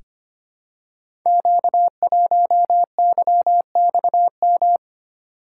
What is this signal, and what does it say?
Morse code, heard as a keyed tone, about 25 wpm, 700 Hz. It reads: Q1YXM
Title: Code Groups - 5 Letter:Numbers - 25wpm